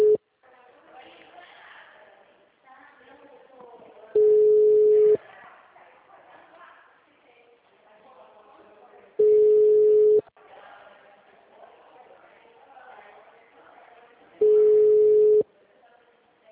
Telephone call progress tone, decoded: ringback tone

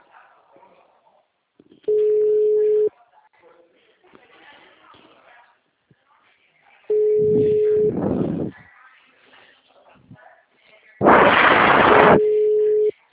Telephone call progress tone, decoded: ringback tone

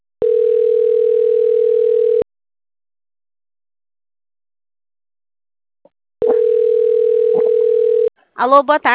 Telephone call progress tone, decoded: ringback tone

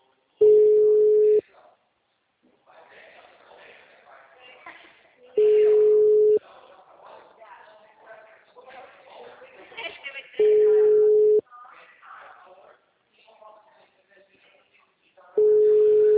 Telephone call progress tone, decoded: ringback tone

